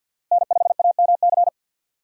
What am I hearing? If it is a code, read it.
Morse code, heard as a keyed tone, about 52 wpm, 700 Hz. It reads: N5AMC
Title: USA Call Signs - 5 Characters - Repeated 2x - 50wpm